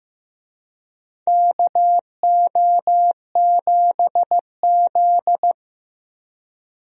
Morse code, heard as a keyed tone, about 16 wpm, 700 Hz. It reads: KO7Z
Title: USA Call Signs - 4 Characters - Repeated 2x - 15wpm